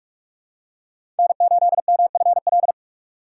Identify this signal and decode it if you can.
Morse code, heard as a keyed tone, about 46 wpm, 700 Hz. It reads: N8MFL
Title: USA Call Signs - 5 Characters - 2x Character Spacing - 45wpm